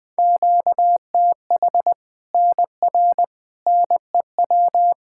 Morse code, heard as a keyed tone, about 21 wpm, 700 Hz. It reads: QTH NR NEW
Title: K1USN SST QSO Practice 20wpm